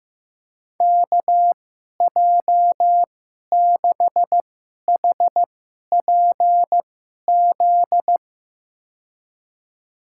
Morse code, heard as a keyed tone, about 15 wpm, spaced as for 11 wpm, 700 Hz. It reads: KJ6HPZ
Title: USA Call Signs - 6 Characters - 2x Character Spacing - 15wpm